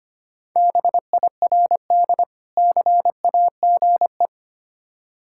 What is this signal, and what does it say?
Morse code, heard as a keyed tone, about 25 wpm, 700 Hz. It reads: BIRD CAGE
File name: practice / 25wpm / Difficult Words with Hints - 25wpm